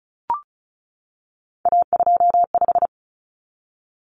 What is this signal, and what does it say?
Morse code, heard as a keyed tone, about 34 wpm, 700 Hz. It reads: A25
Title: Common Cut Numbers - 3-Digit 35wpm